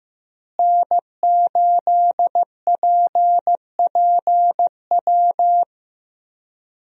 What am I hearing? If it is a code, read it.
Morse code, heard as a keyed tone, about 15 wpm, 700 Hz. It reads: N8PPW
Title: USA Call Signs - 5 Characters - Repeated 3x - 15wpm